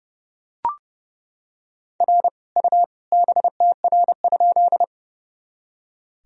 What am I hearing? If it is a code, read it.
Morse code, heard as a keyed tone, about 30 wpm, 700 Hz. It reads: R U BTR?
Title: Sentences from Top 400 Words and Abbreviations 30wpm